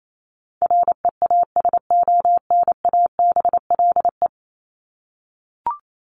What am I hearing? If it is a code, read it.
Morse code, heard as a keyed tone, about 28 wpm, 700 Hz. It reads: REASONABLE
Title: Top 1501-1750 Words 28wpm